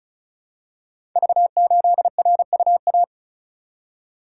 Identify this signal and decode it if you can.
Morse code, heard as a keyed tone, about 36 wpm, 700 Hz. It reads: V8RUA
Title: Code Groups - 5 Letter:Numbers - 35wpm